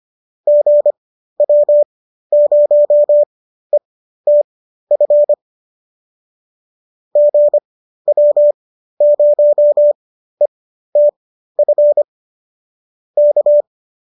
Morse code, heard as a keyed tone, about 25 wpm, spaced as for 13 wpm, 590 Hz. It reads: GW0ETF GW0ETF K